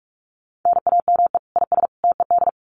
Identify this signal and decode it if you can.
Morse code, heard as a keyed tone, about 44 wpm, 700 Hz. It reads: NAME IS TED
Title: Common QSO Phrases 45wpm